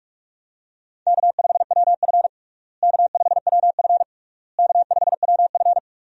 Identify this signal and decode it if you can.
Morse code, heard as a keyed tone, about 46 wpm, 700 Hz. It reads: K5WF K5WF K5WF
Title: USA Call Signs - 4 Characters - Repeated 3x - 45wpm